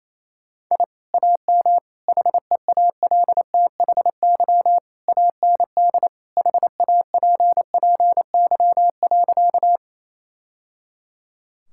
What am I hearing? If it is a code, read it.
Morse code, heard as a keyed tone, about 27 wpm, 700 Hz. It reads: I AM HEALTHY AND HAPPY.
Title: Sentences from Top 1000 Words - Review Part 5 - 28wpm